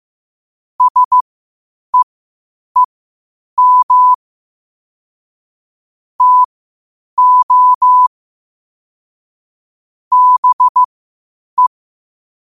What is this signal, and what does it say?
Morse code, heard as a keyed tone, about 15 wpm, spaced as for 8 wpm, 1000 Hz. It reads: SEEM TO BE
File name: qotdfiles / 8wpm